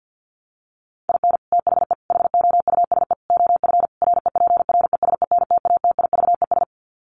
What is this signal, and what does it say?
Morse code, heard as a keyed tone, about 50 wpm, 700 Hz. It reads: IN THE HOUSE OF REPRESENTATIVES
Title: Sets of 5 Words Encore 50wpm